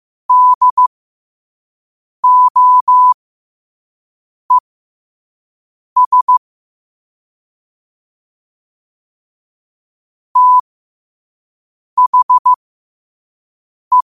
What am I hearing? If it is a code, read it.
Morse code, heard as a keyed tone, about 15 wpm, spaced as for 5 wpm, 1000 Hz. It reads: DOES THE
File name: qotdfiles / 5wpm